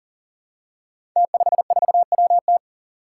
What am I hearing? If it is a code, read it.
Morse code, heard as a keyed tone, about 39 wpm, 700 Hz. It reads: T54WT
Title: Five Challenging Letter-Number Combinations 40wpm